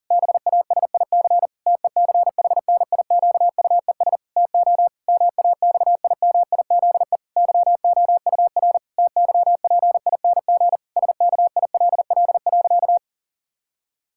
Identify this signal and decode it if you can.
Morse code, heard as a keyed tone, about 40 wpm, 700 Hz. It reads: BASIC TECHNIQUES TO MAXIMIZE YOUR TYPING SKILL.
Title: Pangram Challenge v2 - 40wpm